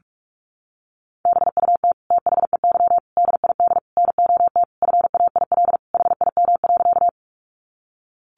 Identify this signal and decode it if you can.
Morse code, heard as a keyed tone, about 44 wpm, 700 Hz. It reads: BUT THEY DID NOT FAIL HIM.
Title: Sentences from Top 1100 Words 45wpm